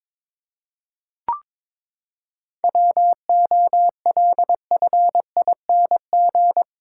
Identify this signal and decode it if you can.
Morse code, heard as a keyed tone, about 22 wpm, 700 Hz. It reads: WOLFING
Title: Pangram Challenge v2 - 22wpm